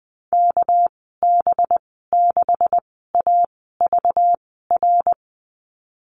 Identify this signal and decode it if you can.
Morse code, heard as a keyed tone, about 20 wpm, spaced as for 15 wpm, 700 Hz. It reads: KB6AVR
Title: 219 Lesson 35g - Number 0 - Call Signs 20wpm